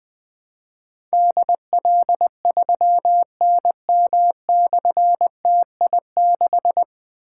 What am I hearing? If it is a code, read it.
Morse code, heard as a keyed tone, about 21 wpm, 700 Hz. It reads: DL3NM/TI6
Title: Difficult Call Signs 20wpm